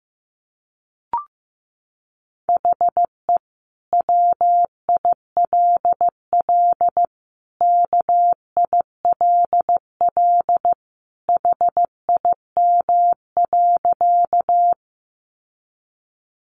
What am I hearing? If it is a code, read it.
Morse code, heard as a keyed tone, about 15 wpm, 700 Hz. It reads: HE WILL KILL HIM.
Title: Sentences from Top 1400 Words 15wpm